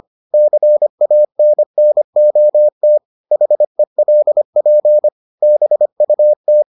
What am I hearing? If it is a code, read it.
Morse code, heard as a keyed tone, about 25 wpm, 600 Hz. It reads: CANNOT HELP BUT